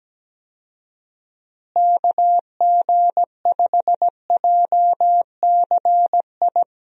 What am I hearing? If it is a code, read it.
Morse code, heard as a keyed tone, about 17 wpm, 700 Hz. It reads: KG5JCI
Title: USA Call Signs - 6 Characters - Repeated 3x - 17wpm